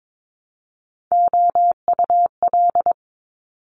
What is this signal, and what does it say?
Morse code, heard as a keyed tone, about 22 wpm, 700 Hz. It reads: OUL
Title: Top Three Letter Combinations in Words 22wpm